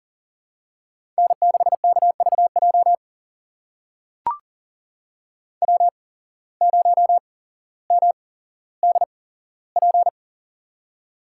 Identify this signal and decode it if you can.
Morse code, heard as a keyed tone, about 40 wpm, spaced as for 17 wpm, 700 Hz. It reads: N6KVJ W0MDP
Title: USA Call Signs - 5 Characters - 8x Character Spacing - 40wpm